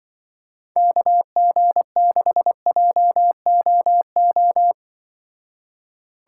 Morse code, heard as a keyed tone, about 24 wpm, 700 Hz. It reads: KG6JOO
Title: Speed Racing - USA 2x3 Call Signs 20wpm